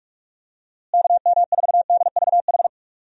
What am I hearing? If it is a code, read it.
Morse code, heard as a keyed tone, about 41 wpm, 700 Hz. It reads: KM4DVH
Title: USA Call Signs - 6 Characters - Repeated 3x - 45wpm